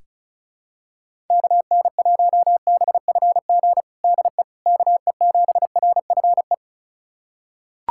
Morse code, heard as a keyed tone, about 34 wpm, 700 Hz. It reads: KN1BFG DE KE7RFE
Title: USA Callsign DE Callsign - 6 Characters - 35wpm